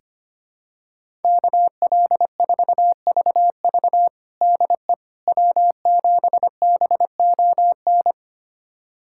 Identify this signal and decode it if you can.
Morse code, heard as a keyed tone, about 25 wpm, 700 Hz. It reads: KL4VV DE W7BON
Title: USA Callsign DE Callsign - 5 Characters - 25wpm